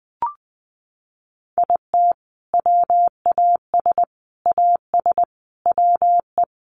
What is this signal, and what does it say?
Morse code, heard as a keyed tone, about 20 wpm, 700 Hz. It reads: IT WAS AS WE
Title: Sentences from Top 1500 Words 20wpm